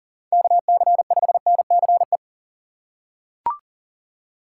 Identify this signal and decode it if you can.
Morse code, heard as a keyed tone, about 37 wpm, 700 Hz. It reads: KC5NCE
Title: USA Call Signs - 6 Characters - 3x Character Spacing - 40wpm